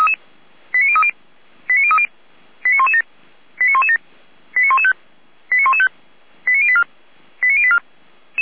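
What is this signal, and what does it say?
Five-tone selective call (ZVEI), heard as quick runs of five tones, tones of about 70 ms. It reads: 89030 89030 89117 89117 89115 89115 89084 89084
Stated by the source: ZVEI five-tone selective call 89512 received via SDR.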